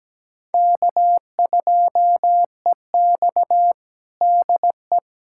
Morse code, heard as a keyed tone, about 17 wpm, 700 Hz. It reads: K2EX DE
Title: USA Callsign DE Callsign 17wpm